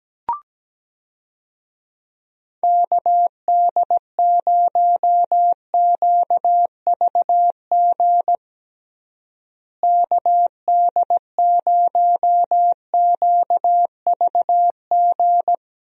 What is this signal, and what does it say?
Morse code, heard as a keyed tone, about 17 wpm, 700 Hz. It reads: KD0QVG KD0QVG
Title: USA Call Signs - 6 Characters - Repeated 3x - 17wpm